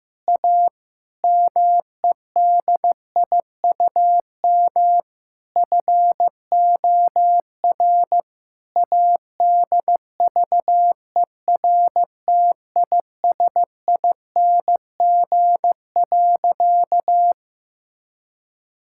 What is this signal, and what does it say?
Morse code, heard as a keyed tone, about 15 wpm, 700 Hz. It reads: A MEDIUM FOR ADVERTISING.
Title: Sentences from Top 1000 Words - Review Part 6 - 15wpm